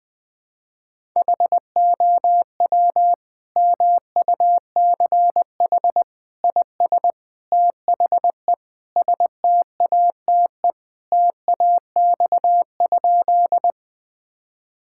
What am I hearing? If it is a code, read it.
Morse code, heard as a keyed tone, about 20 wpm, 700 Hz. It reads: HOW MUCH IS THE STATE TAX?
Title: Sentences from Top 1000 Words - Review Part 5 - 20wpm